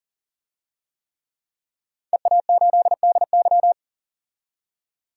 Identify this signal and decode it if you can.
Morse code, heard as a keyed tone, about 40 wpm, 700 Hz. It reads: EA8DY
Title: DXCC-100 Call Signs - 5 Characters - 40wpm